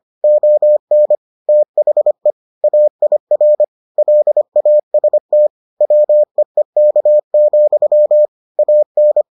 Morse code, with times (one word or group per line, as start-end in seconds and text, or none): 0.2-1.1 ON
1.5-2.3 THE
2.6-3.6 AIR
4.0-5.5 LAST
5.8-8.3 WEEK,
8.6-9.2 AN